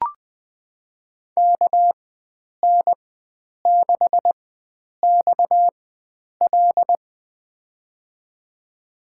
Morse code, heard as a keyed tone, about 20 wpm, spaced as for 9 wpm, 700 Hz. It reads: KN6XL
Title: USA Call Signs - 4x Character Spacing - 20wpm